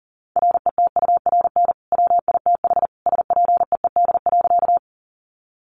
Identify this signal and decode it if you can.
Morse code, heard as a keyed tone, about 40 wpm, 700 Hz. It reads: RETURN WITH SPEED.
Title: Sentences from Top 800 Words 40wpm